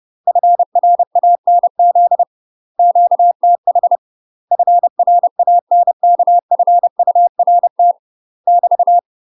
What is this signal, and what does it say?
Morse code, heard as a keyed tone, about 30 wpm, 700 Hz. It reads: FRANZ QTH FRANKFURT <BT>
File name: UK MASTER QSOs TRAINING / 30WPM / UK MASTER QSOs 0001